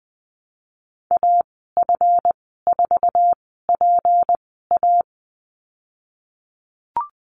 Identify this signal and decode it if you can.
Morse code, heard as a keyed tone, about 19 wpm, spaced as for 14 wpm, 700 Hz. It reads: AF4PA
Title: 179 Lesson 29g - Slash - Call Signs 20wpm